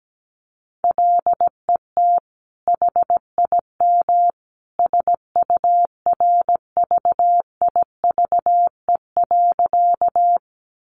Morse code, heard as a keyed tone, about 17 wpm, 700 Hz. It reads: LET HIM SURVIVE.
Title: Sentences from Top 1300 Words 17wpm